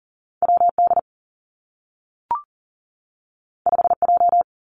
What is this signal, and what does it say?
Morse code, heard as a keyed tone, about 42 wpm, 700 Hz. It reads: W D 5 J
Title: Two Challenging Letter-Number Combinations 40wpm